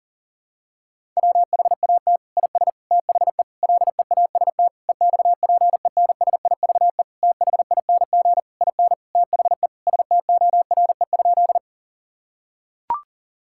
Morse code, heard as a keyed tone, about 37 wpm, 700 Hz. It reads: WHAT IS THE LEAST EXPENSIVE THING IN THE STORE?
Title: Sentences from Top 1000 Words - Review Part 6 - 40wpm